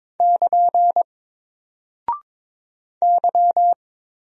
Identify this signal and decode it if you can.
Morse code, heard as a keyed tone, about 22 wpm, 700 Hz. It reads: <KN> Y
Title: Single Character (Letter:Number:Punctuation:Prosign) 22wpm